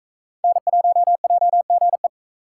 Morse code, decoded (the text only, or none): N1JGE